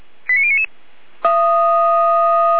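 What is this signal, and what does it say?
Five-tone selective call (ZVEI), heard as quick runs of five tones, tones of about 80 ms. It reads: 89099
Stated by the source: ZVEI five-tone selective call 89510 received via SDR.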